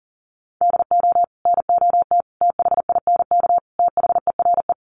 Morse code, heard as a keyed tone, about 40 wpm, 700 Hz. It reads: DO NOT THINK THERE
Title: Sets of 4 Words 40wpm